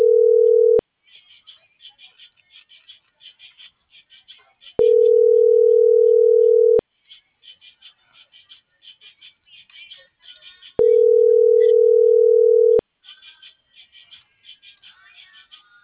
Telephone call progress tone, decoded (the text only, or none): ringback tone